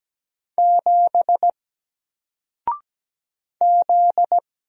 Morse code, heard as a keyed tone, about 17 wpm, 700 Hz. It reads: 7 Z
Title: Single Letter-Number 17wpm